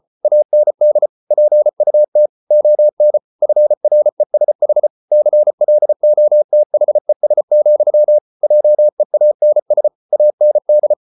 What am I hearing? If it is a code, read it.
Morse code, heard as a keyed tone, about 34 wpm, 600 Hz. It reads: AND PUT ON FRESH CLOTHES, JEANS AND